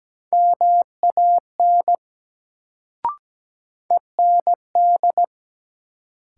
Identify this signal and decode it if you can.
Morse code, heard as a keyed tone, about 17 wpm, 700 Hz. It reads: MAN END
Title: Sentences from Top 200 Words - New Word Drill - 17wpm